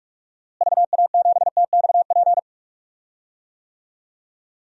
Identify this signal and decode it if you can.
Morse code, heard as a keyed tone, about 42 wpm, 700 Hz. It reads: VA7TXP
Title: DXCC-100 Call Signs - 6 Characters - Repeated 2x - 45wpm